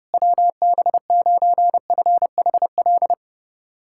Morse code, heard as a keyed tone, about 30 wpm, 700 Hz. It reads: WB9FHL
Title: USA Callsign DE Callsign - 6 Characters - 30wpm